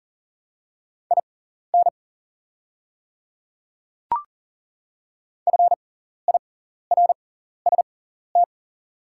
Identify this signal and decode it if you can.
Morse code, heard as a keyed tone, about 40 wpm, spaced as for 14 wpm, 700 Hz. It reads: IN FIRST
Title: 6x Character Spacing - Top 300 Words - 40wpm